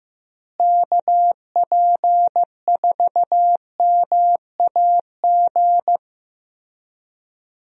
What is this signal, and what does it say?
Morse code, heard as a keyed tone, about 15 wpm, 700 Hz. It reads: KP4MAG
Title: USA Call Signs - 6 Characters - Repeated 3x - 15wpm